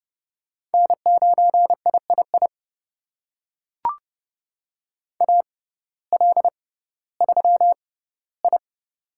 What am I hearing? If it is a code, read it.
Morse code, heard as a keyed tone, about 31 wpm, spaced as for 17 wpm, 700 Hz. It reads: N9III AL3I
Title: USA Call Signs - 6x Character Spacing - 30wpm